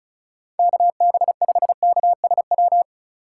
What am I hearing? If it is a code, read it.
Morse code, heard as a keyed tone, about 36 wpm, 690 Hz. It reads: KB5KSW
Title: USA Call Signs - 6 Characters - 2x Character Spacing - 35wpm